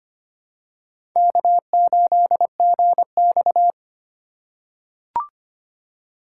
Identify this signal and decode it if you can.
Morse code, heard as a keyed tone, about 25 wpm, 700 Hz. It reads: K8GX
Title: USA Call Signs - 4 Characters - 4x Character Spacing - 25wpm